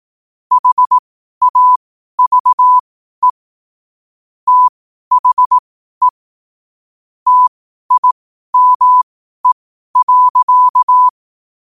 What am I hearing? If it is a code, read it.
Morse code, heard as a keyed tone, about 18 wpm, spaced as for 12 wpm, 1000 Hz. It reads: HAVE THE TIME.
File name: qotdfiles / 12wpm